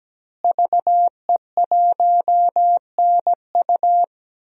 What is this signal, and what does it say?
Morse code, heard as a keyed tone, about 17 wpm, 700 Hz. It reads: VE1NU
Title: DXCC-100 Call Signs - 5 Characters - Repeated 2x - 17wpm